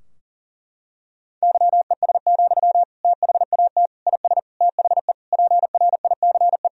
Morse code, heard as a keyed tone, about 40 wpm, 700 Hz. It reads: YES, THAT IS THE PRICE
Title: Sentences from Top 1000 Words - Review Part 4 - 40wpm